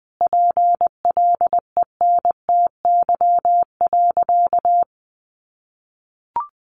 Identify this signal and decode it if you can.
Morse code, heard as a keyed tone, about 20 wpm, 700 Hz. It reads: PLENTY.
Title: Sentences from Top 1300 Words 20wpm